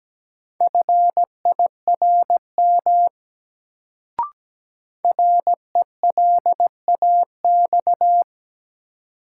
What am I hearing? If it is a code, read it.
Morse code, heard as a keyed tone, about 17 wpm, 700 Hz. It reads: FIRM RELAX
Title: Sentences from Top 1400 Words - New Word Drill - 17wpm